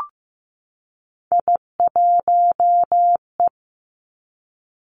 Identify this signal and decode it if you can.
Morse code, heard as a keyed tone, about 15 wpm, 700 Hz. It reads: I1E
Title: Three Challenging Letter-Number Combinations 15wpm